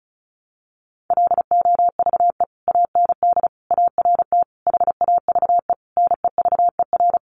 Morse code, heard as a keyed tone, about 34 wpm, 700 Hz. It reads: LOVE AND ART HAVE NEVER
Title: Sentences from Top 1100 Words 35wpm